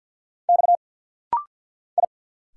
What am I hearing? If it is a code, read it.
Morse code, heard as a keyed tone, about 49 wpm, 700 Hz. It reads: X I
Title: Single Letter-Number - Rapid-Fire 50wpm